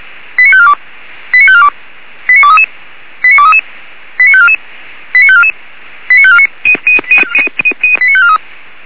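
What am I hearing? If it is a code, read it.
Five-tone selective call (ZVEI), heard as quick runs of five tones, tones of about 80 ms. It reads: 89542 89542 89233 89233 89544 89544 89549 89543
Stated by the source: ZVEI five-tone call received by SDR.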